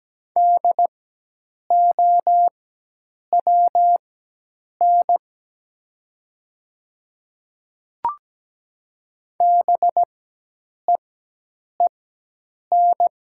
Morse code, heard as a keyed tone, about 17 wpm, spaced as for 8 wpm, 700 Hz. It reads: DOWN BEEN
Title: 4x Character Spacing - Top 100 Words - 17wpm